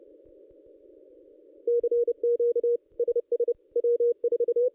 Morse code, heard as a keyed tone, about 30 wpm, 500 Hz. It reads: CQ SS W4
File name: contests / ss-cw-k5tr-op-n5rz -2200